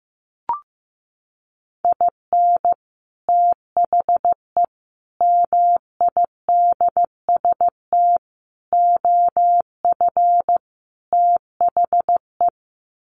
Morse code, heard as a keyed tone, about 15 wpm, 700 Hz. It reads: IN THE MIDST OF THE
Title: Sets of 5 Words 15wpm